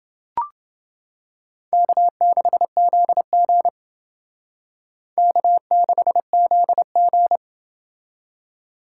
Morse code, heard as a keyed tone, about 29 wpm, 700 Hz. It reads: K6ZG K6ZG